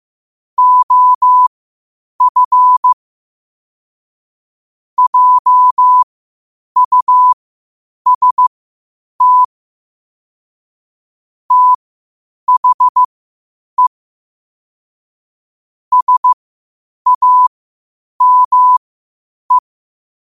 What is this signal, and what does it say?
Morse code, heard as a keyed tone, about 15 wpm, spaced as for 8 wpm, 1000 Hz. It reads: OF JUST THE SAME
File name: qotdfiles / 8wpm